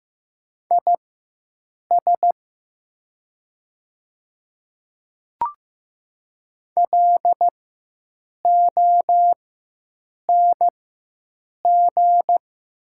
Morse code, heard as a keyed tone, about 15 wpm, spaced as for 7 wpm, 700 Hz. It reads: IS LONG
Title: 4x Character Spacing - Top 500 Words - 15wpm